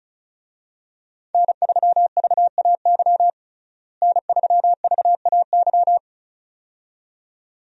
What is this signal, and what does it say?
Morse code, heard as a keyed tone, about 36 wpm, 700 Hz. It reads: N3VAY N3VAY
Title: USA Call Signs - 5 Characters - Repeated 2x - 35wpm